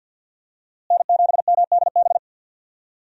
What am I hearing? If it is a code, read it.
Morse code, heard as a keyed tone, about 48 wpm, 700 Hz. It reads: N7MDB
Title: USA Call Signs - 5 Characters - Repeated 3x - 50wpm